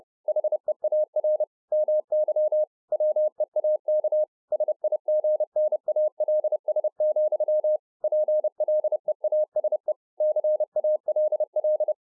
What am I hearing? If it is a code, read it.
Morse code, heard as a keyed tone, about 30 wpm, 610 Hz. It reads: HEAR MY WEAK SIGNALS, PLEASE CALL